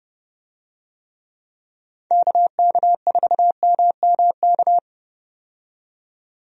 Morse code, heard as a keyed tone, about 30 wpm, 700 Hz. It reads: KK4MMK
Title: USA Call Signs - 6 Characters - 30wpm